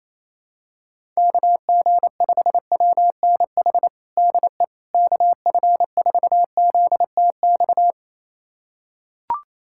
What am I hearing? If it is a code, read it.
Morse code, heard as a keyed tone, about 28 wpm, 700 Hz. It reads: KG5WNH DE KF4ZTX
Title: USA Callsign DE Callsign 28wpm